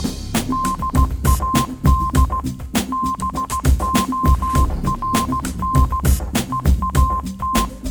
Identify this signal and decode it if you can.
Morse code, heard as a keyed tone, about 16 wpm, 1050 Hz. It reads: DAD BORN EAT
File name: Kirkoid / Kirkoid - Small Stuff (Morse Mix)